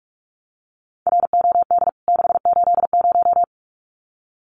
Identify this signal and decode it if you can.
Morse code, heard as a keyed tone, about 44 wpm, 700 Hz. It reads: ROD 680
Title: CWT Exchange 45wpm